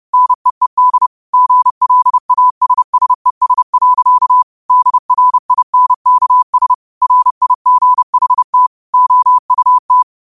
Morse code, heard as a keyed tone, about 30 wpm, 1000 Hz. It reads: NEED GLASSES. DRINKS RIGHT OUT